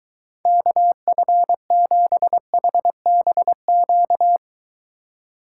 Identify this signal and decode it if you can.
Morse code, heard as a keyed tone, about 23 wpm, 700 Hz. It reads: KF7HBQ